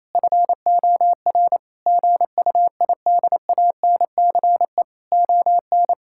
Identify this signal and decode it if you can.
Morse code, heard as a keyed tone, about 27 wpm, 700 Hz. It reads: FOR GUIDANCE ON
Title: Sentences from Top 1600 Words 28wpm